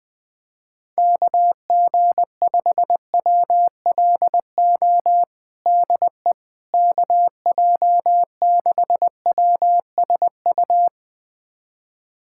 Morse code, heard as a keyed tone, about 20 wpm, 700 Hz. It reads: KG5WLO DE KJ6WSU
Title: USA Callsign DE Callsign 20wpm